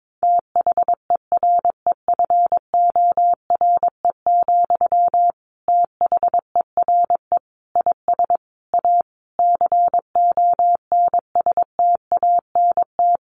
Morse code, heard as a keyed tone, about 22 wpm, 700 Hz. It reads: THEREFORE, THERE IS A CONSTANT
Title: Sentences from Top 1500 Words 22wpm